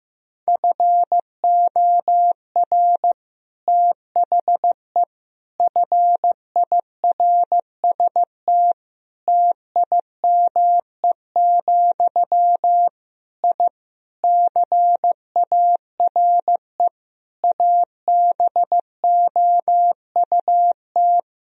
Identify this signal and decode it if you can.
Morse code, heard as a keyed tone, about 15 wpm, 700 Hz. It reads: FOR THE FIRST TIME, I CARE ABOUT
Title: Sentences from Top 1000 Words - Review Part 2 - 15wpm